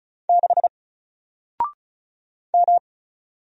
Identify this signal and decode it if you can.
Morse code, heard as a keyed tone, about 36 wpm, 700 Hz. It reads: 6 M
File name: Single-Letter-Number-35wpm